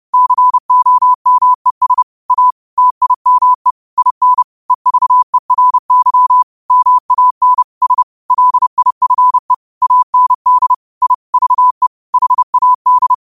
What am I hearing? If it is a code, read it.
Morse code, heard as a keyed tone, about 30 wpm, 1000 Hz. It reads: COMES A TIME IN EVERY MAN S LIFE AND I VE HAD